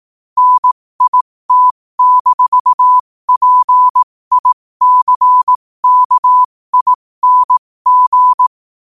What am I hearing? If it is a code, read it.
Morse code, heard as a keyed tone, about 18 wpm, spaced as for 15 wpm, 1000 Hz. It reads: NIT-PICKING
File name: qotdfiles / 15wpm